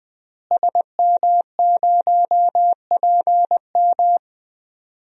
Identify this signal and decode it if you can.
Morse code, heard as a keyed tone, about 20 wpm, 700 Hz. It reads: SM0PM
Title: DXCC-100 Call Signs - 5 Characters - Repeated 2x - 20wpm